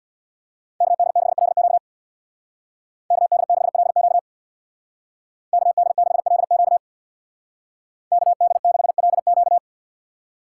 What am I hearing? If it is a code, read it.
Morse code, heard as a keyed tone, about 59 wpm, 700 Hz. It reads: KD6LQ KD6LQ KD6LQ KD6LQ
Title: Speed Racing - USA 2x2 Call Signs 50wpm